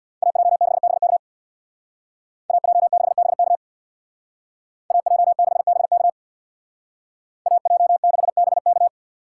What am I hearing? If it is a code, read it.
Morse code, heard as a keyed tone, about 58 wpm, 700 Hz. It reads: AJ6BK AJ6BK AJ6BK AJ6BK
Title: Speed Racing - USA 2x2 Call Signs 50wpm